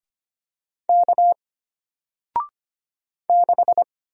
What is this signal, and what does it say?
Morse code, heard as a keyed tone, about 25 wpm, 700 Hz. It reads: K 6
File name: Single-Letter-Number-25wpm